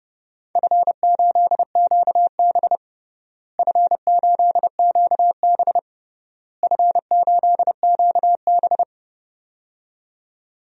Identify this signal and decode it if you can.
Morse code, heard as a keyed tone, about 30 wpm, 700 Hz. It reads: F8QB F8QB F8QB
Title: DXCC-100 Call Signs - 4 Characters - Repeated 3x - 30wpm